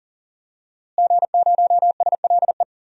Morse code, heard as a keyed tone, about 39 wpm, 700 Hz. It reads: G0SLE
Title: DXCC-100 Call Signs - 5 Characters - Repeated 2x - 40wpm